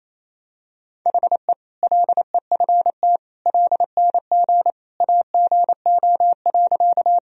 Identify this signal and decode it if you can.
Morse code, heard as a keyed tone, about 28 wpm, 700 Hz. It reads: HE LEFT LNG AGO.
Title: Sentences from Top 500 Words and Abbreviations 28wpm